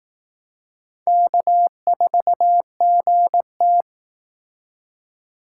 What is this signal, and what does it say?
Morse code, heard as a keyed tone, about 18 wpm, 700 Hz. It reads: K4GT
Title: Speed Racing - USA 1x2 Call Signs 15wpm